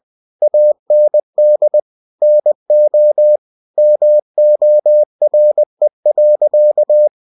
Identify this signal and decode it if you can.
Morse code, heard as a keyed tone, about 20 wpm, 600 Hz. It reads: AND NO MORE.
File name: cw-books / winnie-the-pooh / winnie-20wpm-page-0040